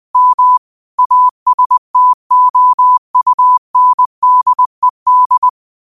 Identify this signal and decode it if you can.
Morse code, heard as a keyed tone, about 20 wpm, 1000 Hz. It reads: M ASTOUNDED
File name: qotdfiles / 20wpm